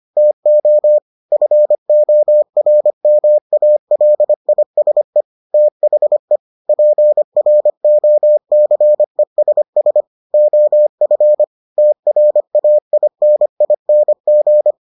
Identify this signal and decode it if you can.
Morse code, heard as a keyed tone, about 25 wpm, 600 Hz. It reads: TO FORMALISE THE PROCESS OF TRAINING